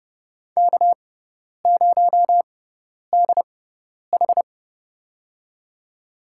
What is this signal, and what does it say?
Morse code, heard as a keyed tone, about 30 wpm, spaced as for 10 wpm, 700 Hz. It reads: K0DH
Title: USA Call Signs - 6x Character Spacing - 30wpm